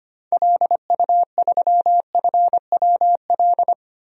Morse code, heard as a keyed tone, about 25 wpm, 700 Hz. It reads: LU3FWL
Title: DXCC-100 Call Signs - 6 Characters - 25wpm